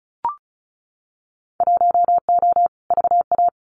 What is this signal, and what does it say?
Morse code, heard as a keyed tone, about 34 wpm, 700 Hz. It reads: 1O VA
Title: Winter Field Day Exchanges 35wpm